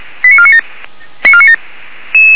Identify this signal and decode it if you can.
Five-tone selective call (ZVEI), heard as quick runs of five tones, tones of about 70 ms. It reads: 89497